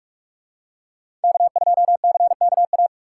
Morse code, heard as a keyed tone, about 43 wpm, 700 Hz. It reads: K2CKA